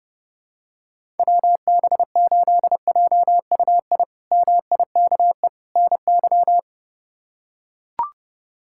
Morse code, heard as a keyed tone, about 32 wpm, 700 Hz. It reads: WB8JUI MIKE NY
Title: K1USN SST QSO Practice - 30wpm